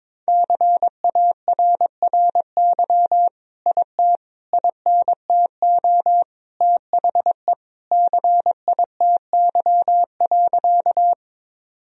Morse code, heard as a keyed tone, about 22 wpm, 700 Hz. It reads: CARRY IT INTO THE CITY.
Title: Sentences from Top 1000 Words - Review Part 1 - 22wpm